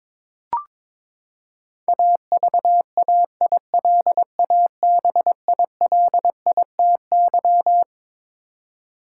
Morse code, heard as a keyed tone, about 22 wpm, 700 Hz. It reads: AVAILABILITY
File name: Twelve Letter Words - 22wpm